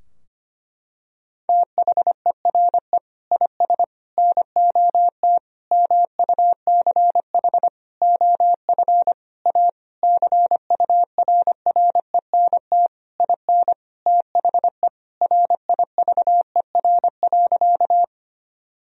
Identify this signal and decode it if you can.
Morse code, heard as a keyed tone, about 25 wpm, 700 Hz. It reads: THERE IS NOT MUCH OF A CURRENT IN THE RIVER.
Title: Sentences from Top 1000 Words - Review Part 4 - 25wpm